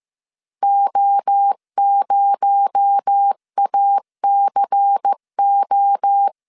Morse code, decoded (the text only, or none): O0ACO